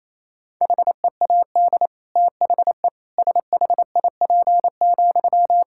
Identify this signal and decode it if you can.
Morse code, heard as a keyed tone, about 28 wpm, 700 Hz. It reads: HEAD THE SHIP,
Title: Sentences from Top 1000 Words - Review Part 5 - 28wpm